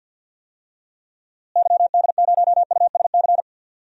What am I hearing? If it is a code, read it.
Morse code, heard as a keyed tone, about 52 wpm, 700 Hz. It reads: YD0USC
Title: DXCC-100 Call Signs - 6 Characters - Repeated 2x - 50wpm